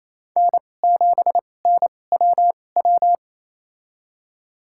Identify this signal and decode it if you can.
Morse code, heard as a keyed tone, about 28 wpm, spaced as for 20 wpm, 700 Hz. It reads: N7NWW
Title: USA Call Signs - 5 Characters - 2x Character Spacing - 28wpm